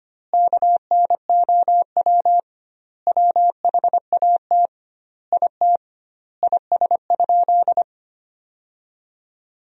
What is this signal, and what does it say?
Morse code, heard as a keyed tone, about 25 wpm, 700 Hz. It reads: KNOW WHAT IT IS?
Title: 2x Word-Spacing - Sentences from Top 100 Words - 25wpm